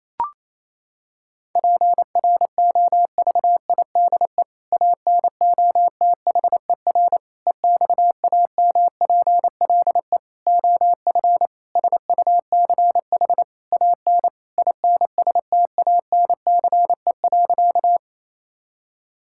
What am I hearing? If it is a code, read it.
Morse code, heard as a keyed tone, about 28 wpm, 700 Hz. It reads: PROVIDE ANOTHER EXAMPLE OF SUCH AN INSTANCE.
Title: Sentences from Top 1000 Words - Review Part 5 - 28wpm